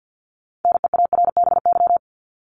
Morse code, heard as a keyed tone, about 50 wpm, 700 Hz. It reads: NEARBY
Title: Top 1501-1750 Words 50wpm